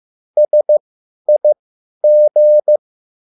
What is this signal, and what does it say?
Morse code, heard as a keyed tone, about 15 wpm, spaced as for 10 wpm, 600 Hz. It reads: SIG